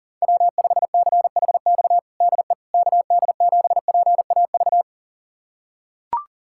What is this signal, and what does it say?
Morse code, heard as a keyed tone, about 40 wpm, 700 Hz. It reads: W5CHX DE KD7PAV
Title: USA Callsign DE Callsign 40wpm